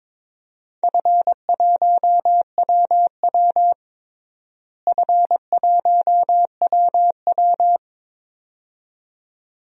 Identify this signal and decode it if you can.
Morse code, heard as a keyed tone, about 22 wpm, 700 Hz. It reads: F1WW F1WW
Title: DXCC-100 Call Signs - 4 Characters - Repeated 2x - 22wpm